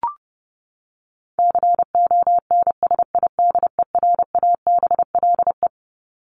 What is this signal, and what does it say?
Morse code, heard as a keyed tone, about 28 wpm, 700 Hz. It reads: CONSIDERABLE
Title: LY Suffix - Top 100 Words with Base Words 30wpm